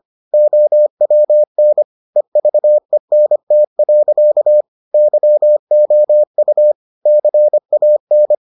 Morse code, heard as a keyed tone, about 25 wpm, 600 Hz. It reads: OWN EVENT. YOU CAN